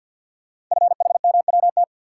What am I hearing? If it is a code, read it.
Morse code, heard as a keyed tone, about 51 wpm, 700 Hz. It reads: FHMWT
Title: Code Groups - 5 Letters - 50wpm